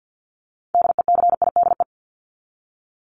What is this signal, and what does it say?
Morse code, heard as a keyed tone, about 50 wpm, 700 Hz. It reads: DECIDE
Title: ED Suffix - Top 100 Words with Base Words 50wpm